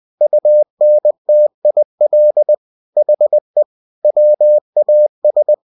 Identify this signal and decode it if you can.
Morse code, heard as a keyed tone, about 20 wpm, 600 Hz. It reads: UNTIL HE WAS
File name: cw-books / winnie-the-pooh / winnie-20wpm-page-0035